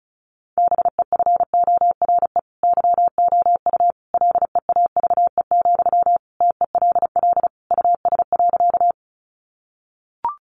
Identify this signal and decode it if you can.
Morse code, heard as a keyed tone, about 34 wpm, 700 Hz. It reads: BEFORE YOU LEAVE, TELL US.
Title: Sentences from Top 400 Words 35wpm